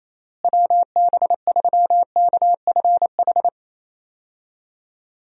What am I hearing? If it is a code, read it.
Morse code, heard as a keyed tone, about 28 wpm, 700 Hz. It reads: WB3KFH
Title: USA Call Signs - 6 Characters - Repeated 2x - 28wpm